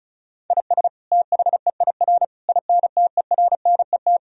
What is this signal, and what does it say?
Morse code, heard as a keyed tone, about 34 wpm, 700 Hz. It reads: IS THEIR INTERNET
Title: Sentences from Top 1000 Words - Review Part 4 - 35wpm